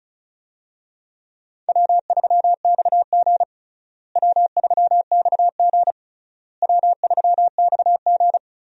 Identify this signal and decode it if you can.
Morse code, heard as a keyed tone, about 36 wpm, 700 Hz. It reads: W3XG W3XG W3XG
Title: USA Call Signs - 4 Characters - Repeated 3x - 35wpm